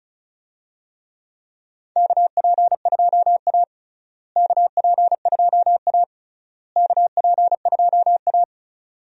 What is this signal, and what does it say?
Morse code, heard as a keyed tone, about 36 wpm, 700 Hz. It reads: KP2A KP2A KP2A
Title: DXCC-100 Call Signs - 4 Characters - Repeated 3x - 35wpm